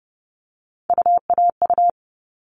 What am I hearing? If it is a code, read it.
Morse code, heard as a keyed tone, about 30 wpm, 700 Hz. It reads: UAU
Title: All Cut Numbers - 3-Digit 30wpm